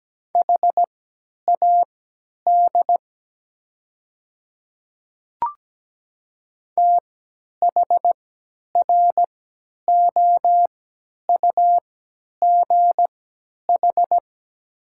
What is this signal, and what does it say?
Morse code, heard as a keyed tone, about 17 wpm, spaced as for 10 wpm, 700 Hz. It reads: HAD THROUGH
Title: 3x Character Spacing - Top 200 Words - 17wpm